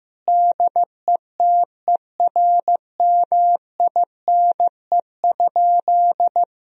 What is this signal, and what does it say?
Morse code, heard as a keyed tone, about 15 wpm, 700 Hz. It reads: DETERMINE?
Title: Sentences from Top 1000 Words - Review Part 4 - 15wpm